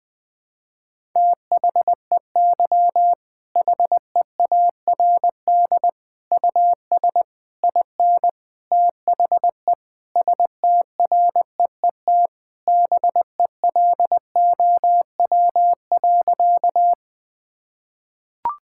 Morse code, heard as a keyed tone, about 21 wpm, 700 Hz. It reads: THEY HEARD US IN THE STREET BELOW.
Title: Sentences from Top 1000 Words - Review Part 5 - 20wpm